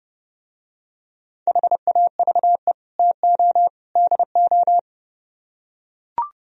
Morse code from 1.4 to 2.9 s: HAVE

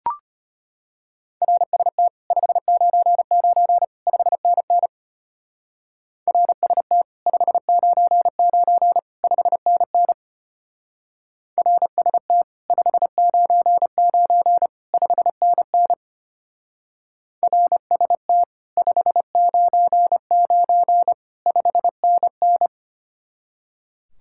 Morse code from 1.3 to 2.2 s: RST

